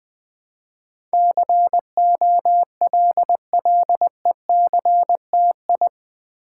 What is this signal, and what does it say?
Morse code, heard as a keyed tone, about 20 wpm, 700 Hz. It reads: COLLECTI
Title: Build A Word - 4 Letters At A Time - 10-Letter Words - 20wpm